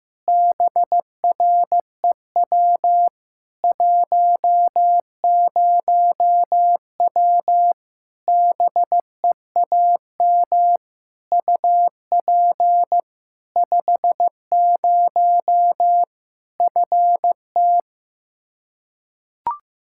Morse code, heard as a keyed tone, about 15 wpm, 700 Hz. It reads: BREW 10W BEAM UP 50 FT
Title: Practice QSO - Exchange 2 - Info - 15wpm